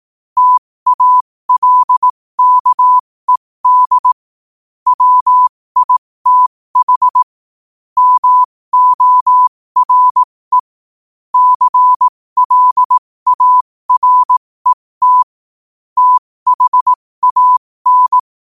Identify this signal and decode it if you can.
Morse code, heard as a keyed tone, about 18 wpm, spaced as for 15 wpm, 1000 Hz. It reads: TALKED WITH MORE CLARET THAN
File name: qotdfiles / 15wpm